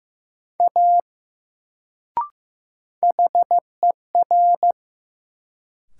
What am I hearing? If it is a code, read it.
Morse code, heard as a keyed tone, about 15 wpm, 700 Hz. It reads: A HER
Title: Sentences from Top 100 Words - New Word Drill - 15wpm